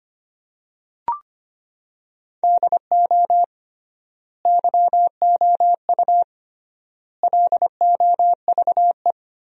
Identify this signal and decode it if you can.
Morse code, heard as a keyed tone, about 25 wpm, 700 Hz. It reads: DO YOU LOVE
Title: 3x Word-Spacing - Sentences from Top 400 Words - 25wpm